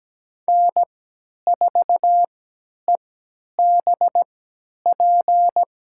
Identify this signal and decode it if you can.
Morse code, heard as a keyed tone, about 17 wpm, spaced as for 10 wpm, 700 Hz. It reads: N4EBP
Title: USA Call Signs - 3x Character Spacing - 17wpm